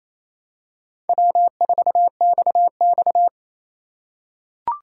Morse code, heard as a keyed tone, about 27 wpm, 700 Hz. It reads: W4XX
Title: USA Call Signs - 3x Character Spacing - 28wpm